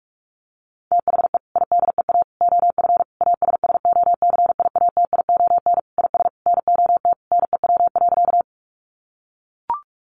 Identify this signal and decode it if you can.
Morse code, heard as a keyed tone, about 44 wpm, 700 Hz. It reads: THE IDEA OF ASSOCIATION IS NOT NEW.
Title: Sentences from Top 1100 Words 45wpm